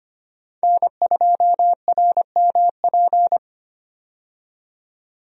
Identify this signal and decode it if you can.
Morse code, heard as a keyed tone, about 25 wpm, 700 Hz. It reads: N2RMP
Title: USA Call Signs - 5 Characters - 25wpm